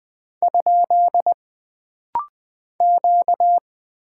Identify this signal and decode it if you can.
Morse code, heard as a keyed tone, about 20 wpm, 700 Hz. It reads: ? Q
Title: Single Character (Letter:Number:Punctuation:Prosign) - Rapid-Fire - 20wpm